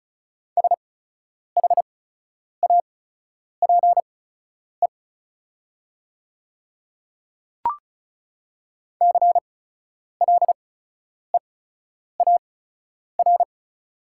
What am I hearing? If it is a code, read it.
Morse code, heard as a keyed tone, about 36 wpm, spaced as for 10 wpm, 690 Hz. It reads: SHAPE CLEAR
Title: 8x Character Spacing - Top 500 Words - 35wpm